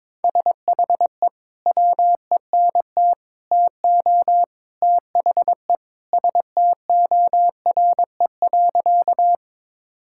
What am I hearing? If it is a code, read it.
Morse code, heard as a keyed tone, about 22 wpm, 700 Hz. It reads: SHE WENT TO THE STORE.
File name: practice / 22wpm / Sentences with 3-Word Phrases - 22wpm